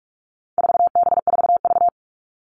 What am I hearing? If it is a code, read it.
Morse code, heard as a keyed tone, about 44 wpm, 700 Hz. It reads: 4B4V
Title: All Cut Numbers - 4-Digit 45wpm